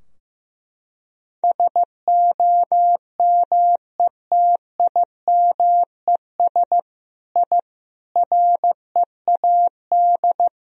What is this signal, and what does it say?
Morse code, heard as a keyed tone, about 15 wpm, 700 Hz. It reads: SOMETIMES I READ
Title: Sentences from Top 1000 Words - Review Part 3 - 15wpm